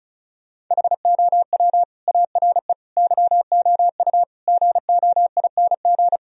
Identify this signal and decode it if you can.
Morse code, heard as a keyed tone, about 43 wpm, 700 Hz. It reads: HOW ARE YOU GOING
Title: Speed-Racing - Sets of 5 Words 35wpm